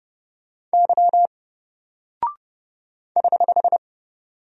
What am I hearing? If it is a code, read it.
Morse code, heard as a keyed tone, about 31 wpm, 700 Hz. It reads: Y <HH>
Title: Single Character (Letter:Number:Punctuation:Prosign) 30wpm